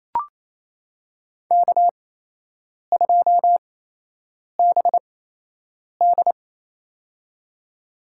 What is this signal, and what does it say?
Morse code, heard as a keyed tone, about 28 wpm, spaced as for 8 wpm, 700 Hz. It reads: K2BD
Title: USA Call Signs - 4 Characters - 8x Character Spacing - 28wpm